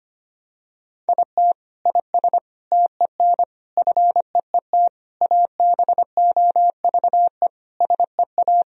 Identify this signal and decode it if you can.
Morse code, heard as a keyed tone, about 25 wpm, 700 Hz. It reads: IT IS TEN FEET ABOVE SEA
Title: Sentences from Top 1000 Words - Review Part 4 - 25wpm